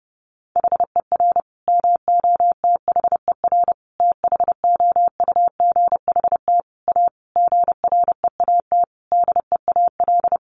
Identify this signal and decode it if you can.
Morse code, heard as a keyed tone, about 30 wpm, 700 Hz. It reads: HER MOTHER THOUGHT A GREAT DEAL
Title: Sentences from Top 800 Words 30wpm